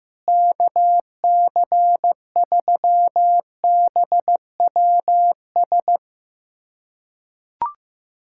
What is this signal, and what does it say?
Morse code, heard as a keyed tone, about 16 wpm, 700 Hz. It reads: KC3BWS
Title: USA Callsign DE Callsign 15wpm